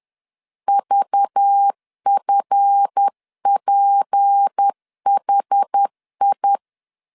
Morse code, heard as a keyed tone, about 10 wpm, 800 Hz. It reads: VFPHI
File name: LEZIONI / 7 GF9Z / 7 GF 10wpm 2514